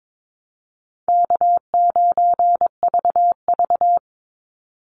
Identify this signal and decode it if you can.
Morse code, heard as a keyed tone, about 22 wpm, 700 Hz. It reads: K9VV
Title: Call Signs Repeated 3x - 22wpm